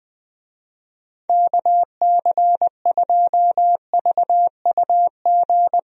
Morse code, heard as a keyed tone, about 20 wpm, 700 Hz. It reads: KC2VUG
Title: USA Call Signs - 6 Characters - Repeated 3x - 20wpm